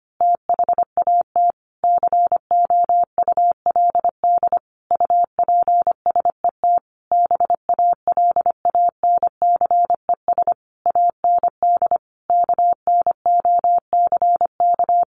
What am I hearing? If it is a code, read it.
Morse code, heard as a keyed tone, about 25 wpm, 700 Hz. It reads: THAT COULD UPSET BALANCES AND KNOCK
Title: A Sound Of Thunder 25wpm